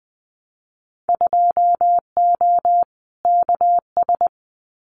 Morse code, heard as a keyed tone, about 20 wpm, 700 Hz. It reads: 2O KS
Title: Winter Field Day Exchanges Encore 20wpm